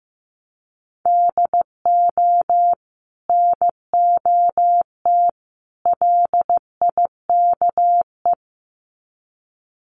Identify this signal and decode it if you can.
Morse code, heard as a keyed tone, about 15 wpm, 700 Hz. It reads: DO NOT LIKE
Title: Sets of 3 Words 15wpm